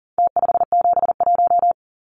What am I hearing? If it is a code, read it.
Morse code, heard as a keyed tone, about 40 wpm, 700 Hz. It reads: T571
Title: Numbers - Contest Style - 4-Digit 40wpm